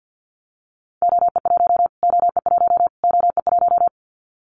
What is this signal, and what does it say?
Morse code, heard as a keyed tone, about 50 wpm, 700 Hz. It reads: OE1 OE1 OE1
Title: Callsign Prefixes for Top 25 DXCC - Repeated 3x - 50wpm